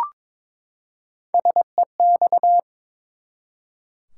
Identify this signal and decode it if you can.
Morse code, heard as a keyed tone, about 22 wpm, 700 Hz. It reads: SEX
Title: Three Letter Words - 22wpm